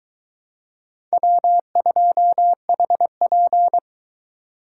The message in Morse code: W2HP

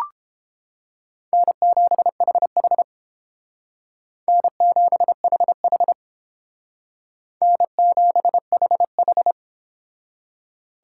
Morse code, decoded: N7HH N7HH N7HH

